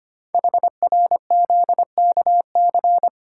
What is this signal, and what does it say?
Morse code, heard as a keyed tone, about 25 wpm, 700 Hz. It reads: HRZKC
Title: Code Groups - 5 Letters - 25wpm